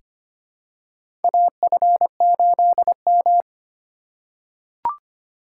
Morse code, heard as a keyed tone, about 25 wpm, 700 Hz. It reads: AF8M
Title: USA Call Signs - 4 Characters - 4x Character Spacing - 25wpm